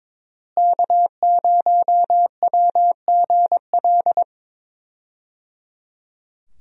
Morse code, heard as a keyed tone, about 22 wpm, 700 Hz. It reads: K0WGL
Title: USA Call Signs - 5 Characters - Repeated 3x - 22wpm